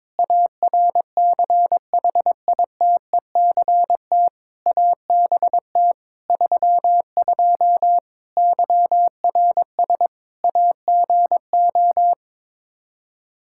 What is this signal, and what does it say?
Morse code, heard as a keyed tone, about 22 wpm, 700 Hz. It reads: ARCHITECT ABT 32 YRS AGO
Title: Practice QSO - Exchange 3 - More Info - 22wpm